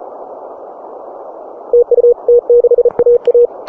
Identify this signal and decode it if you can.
Morse code, heard as a keyed tone, about 35 wpm, 480 Hz. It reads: TU T6AA